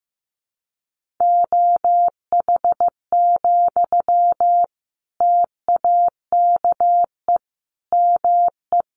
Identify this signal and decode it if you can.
Morse code, heard as a keyed tone, about 15 wpm, 700 Hz. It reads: OH, TAKE ME
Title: Sentences from Top 1400 Words 15wpm